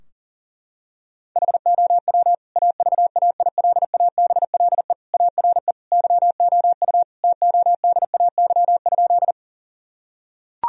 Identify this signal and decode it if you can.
Morse code, heard as a keyed tone, about 40 wpm, 700 Hz. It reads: HOW AVAILABLE ARE YOU TODAY?
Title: Sentences from Top 1000 Words - Review Part 3 - 40wpm